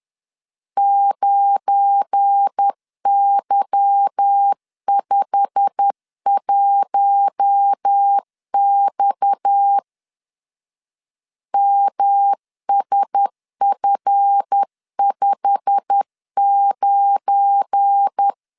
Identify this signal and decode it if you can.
Morse code, heard as a keyed tone, about 10 wpm, 800 Hz. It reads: 9Y51X MSF59
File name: LEZIONI / 9 2867 10wpm 2733